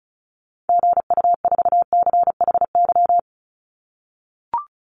Morse code, heard as a keyed tone, about 34 wpm, 700 Hz. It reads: GU4CHY
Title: Call Signs - 35wpm